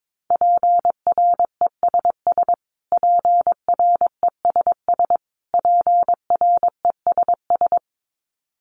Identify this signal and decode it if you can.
Morse code, heard as a keyed tone, about 22 wpm, 700 Hz. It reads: PRESS PRESS PRESS
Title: Top 300 Words Repeated 22wpm